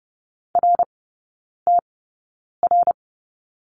Morse code, heard as a keyed tone, about 30 wpm, spaced as for 9 wpm, 700 Hz. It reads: RTR
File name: course / 30:10wpm / 087 Lesson 15e - Number 5 - Three Characters 30wpm